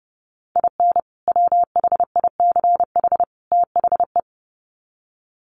Morse code, heard as a keyed tone, about 30 wpm, 700 Hz. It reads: IN WHICH THE
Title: Sets of 3 Words 30wpm